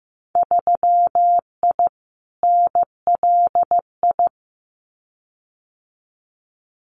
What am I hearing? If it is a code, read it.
Morse code, heard as a keyed tone, about 15 wpm, 700 Hz. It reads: 3I NLI
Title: Winter Field Day Exchanges 15wpm